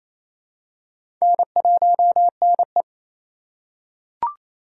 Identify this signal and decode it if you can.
Morse code, heard as a keyed tone, about 28 wpm, 700 Hz. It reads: N1NE
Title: USA Call Signs - 2x Character Spacing - 28wpm